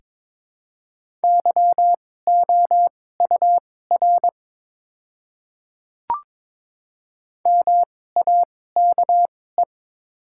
Morse code, heard as a keyed tone, about 22 wpm, spaced as for 16 wpm, 700 Hz. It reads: YOUR MAKE
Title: 2x Character Spacing - Top 100 Words - 22wpm